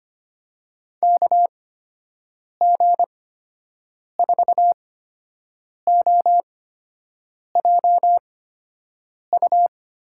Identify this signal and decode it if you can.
Morse code, heard as a keyed tone, about 25 wpm, spaced as for 7 wpm, 700 Hz. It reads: KG4OJU
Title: USA Call Signs - 6 Characters - 8x Character Spacing - 25wpm